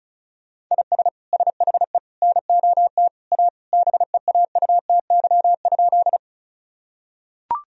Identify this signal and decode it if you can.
Morse code, heard as a keyed tone, about 36 wpm, 700 Hz. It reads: IS SHE NOT A BEAUTY?
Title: Sentences from Top 1000 Words - Review Part 2 - 35wpm